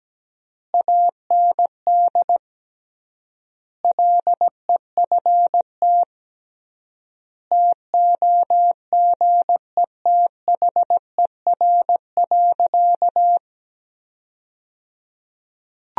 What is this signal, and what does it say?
Morse code, heard as a keyed tone, about 17 wpm, 700 Hz. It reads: AND LEFT TOGETHER.
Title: 3x Word-Spacing - Sentences from Top 400 Words - 17wpm